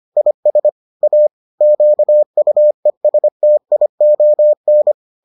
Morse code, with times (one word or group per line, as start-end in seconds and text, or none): 0.2-0.7 IS
1.0-1.3 A
1.6-4.9 QUESTION